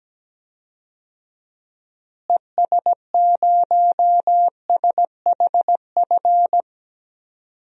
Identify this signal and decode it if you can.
Morse code, heard as a keyed tone, about 17 wpm, 700 Hz. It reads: ES0SHF
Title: DXCC-100 Call Signs - 6 Characters - Repeated 3x - 17wpm